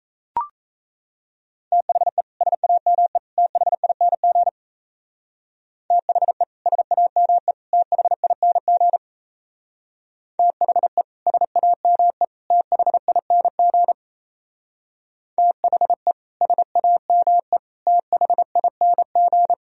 Morse code, heard as a keyed tone, about 34 wpm, 700 Hz. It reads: THE SAME THING THE SAME THING THE SAME THING THE SAME THING
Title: Speed-Racing - Sets of 3 Words 28wpm